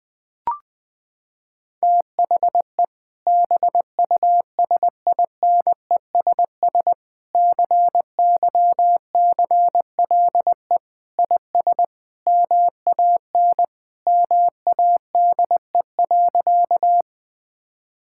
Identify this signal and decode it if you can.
Morse code, heard as a keyed tone, about 21 wpm, 700 Hz. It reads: THE BUSINESS CYCLE IS MAN MADE.
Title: Sentences from Top 1000 Words - Review Part 6 - 20wpm